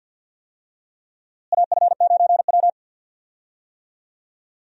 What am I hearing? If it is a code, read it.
Morse code, heard as a keyed tone, about 52 wpm, 700 Hz. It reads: AF9W
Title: USA Call Signs - 4 Characters - Repeated 2x - 50wpm